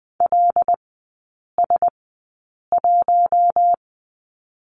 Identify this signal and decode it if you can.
Morse code, heard as a keyed tone, about 20 wpm, spaced as for 8 wpm, 700 Hz. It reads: LS1
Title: 184 Lesson 30e - Letter b - Three Characters 20wpm